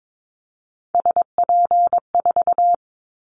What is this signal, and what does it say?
Morse code, heard as a keyed tone, about 22 wpm, 700 Hz. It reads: SP4
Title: Callsign Prefixes for Top 25 DXCC - 22wpm